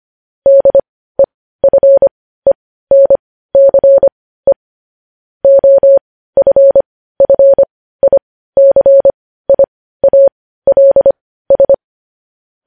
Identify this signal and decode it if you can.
Morse code, heard as a keyed tone, about 25 wpm, spaced as for 15 wpm, 550 Hz. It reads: DEFENCE OFFICIALS
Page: News Headlines in Morse Code at 15 WPM